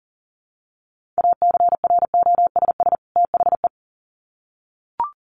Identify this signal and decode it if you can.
Morse code, heard as a keyed tone, about 40 wpm, 700 Hz. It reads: ACROSS THE
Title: Sets of 2 Words 40wpm